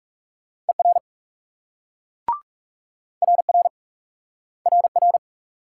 Morse code, heard as a keyed tone, about 38 wpm, 700 Hz. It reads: ER RR RR